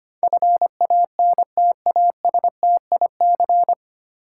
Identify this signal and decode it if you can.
Morse code, heard as a keyed tone, about 25 wpm, 690 Hz. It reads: FANTASTIC
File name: practice / 25wpm / 9 Letter Words - First 3 Letters Given - 25wpm